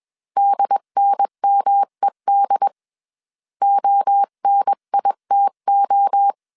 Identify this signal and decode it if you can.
Morse code, heard as a keyed tone, about 21 wpm, 800 Hz. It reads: DNMED ONITO
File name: 20wpm